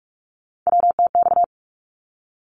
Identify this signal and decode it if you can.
Morse code, heard as a keyed tone, about 44 wpm, 700 Hz. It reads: WTX
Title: ARRL Sections 45wpm